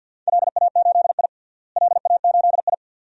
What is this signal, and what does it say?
Morse code, heard as a keyed tone, about 49 wpm, 700 Hz. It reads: LA8I LA8I
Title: DXCC-100 Call Signs - 4 Characters - Repeated 2x - 50wpm